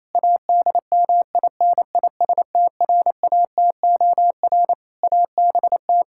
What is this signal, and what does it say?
Morse code, heard as a keyed tone, about 28 wpm, 700 Hz. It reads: ADMINISTRATOR ABT
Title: Practice QSO - Exchange 3 - More Info - 28wpm